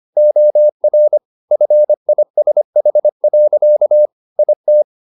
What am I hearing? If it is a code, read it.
Morse code, heard as a keyed tone, about 25 wpm, 600 Hz. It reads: OR FISH. IT
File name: the old man and the sea part 1 25wpm